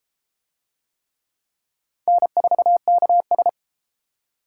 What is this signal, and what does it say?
Morse code, heard as a keyed tone, about 33 wpm, 700 Hz. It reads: N4KS